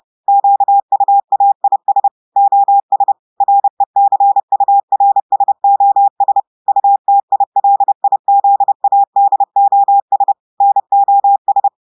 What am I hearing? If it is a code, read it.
Morse code, heard as a keyed tone, about 30 wpm, 800 Hz. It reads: QUAIS OS RECURSOS UTILIZADOS NOS
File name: audios / portugais / portugais texte en clair 2 vitesse 30 mots minutes